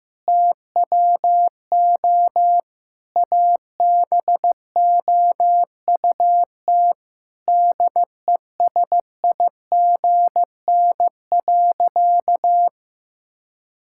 Morse code, heard as a keyed tone, about 15 wpm, 700 Hz. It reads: TWO ABOUT DESIGN.
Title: Sentences from Top 1000 Words - Review Part 5 - 15wpm